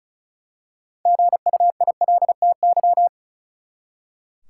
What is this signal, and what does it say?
Morse code, heard as a keyed tone, about 34 wpm, 700 Hz. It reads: GUILTY
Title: Six Letter Words - 35wpm